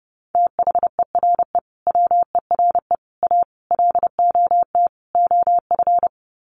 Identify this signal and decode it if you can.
Morse code, heard as a keyed tone, about 30 wpm, 700 Hz. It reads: THERE WERE A LOT OF
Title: Sets of 5 Words 30wpm